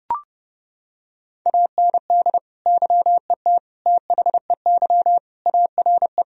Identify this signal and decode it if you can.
Morse code, heard as a keyed tone, about 30 wpm, 700 Hz. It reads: AND YET THEY ARE
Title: Sentences from Top 1000 Words - Review Part 4 - 30wpm